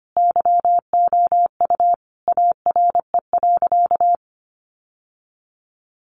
Morse code, heard as a keyed tone, about 25 wpm, 700 Hz. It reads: YOU ARE.
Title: Sentences from Top 1100 Words 25wpm